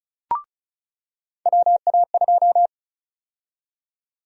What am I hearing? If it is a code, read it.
Morse code, heard as a keyed tone, about 36 wpm, 700 Hz. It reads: WA2
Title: USA Call Sign Prefixes - 35wpm